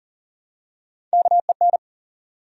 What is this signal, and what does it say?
Morse code, heard as a keyed tone, about 40 wpm, 700 Hz. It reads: KEN
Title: USA Common Names - Max 3 Letters - 40wpm